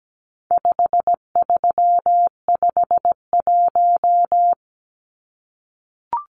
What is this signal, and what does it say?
Morse code, heard as a keyed tone, about 17 wpm, 700 Hz. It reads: 5351
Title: Numbers - 4-Digit 17wpm